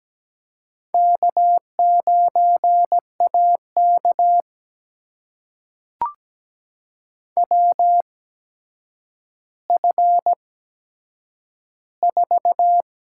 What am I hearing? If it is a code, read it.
Morse code, heard as a keyed tone, about 17 wpm, spaced as for 8 wpm, 700 Hz. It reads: K9AK WF4
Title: USA Call Signs - 4 Characters - 8x Character Spacing - 17wpm